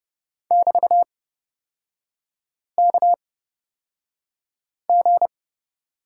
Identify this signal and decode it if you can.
Morse code, heard as a keyed tone, about 30 wpm, spaced as for 5 wpm, 700 Hz. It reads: <BT>KG
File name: Lesson 36e - BT - Three Characters 30wpm